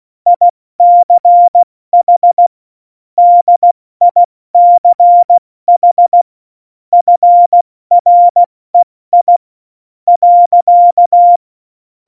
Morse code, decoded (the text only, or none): ICH DICH FREI.